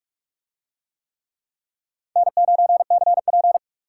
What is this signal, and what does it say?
Morse code, heard as a keyed tone, about 44 wpm, 700 Hz. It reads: N9CP